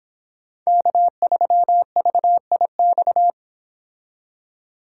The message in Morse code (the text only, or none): K3VIX